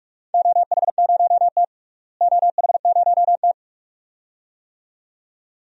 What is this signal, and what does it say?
Morse code, heard as a keyed tone, about 43 wpm, 700 Hz. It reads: OH0T OH0T
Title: DXCC-100 Call Signs - 4 Characters - Repeated 2x - 45wpm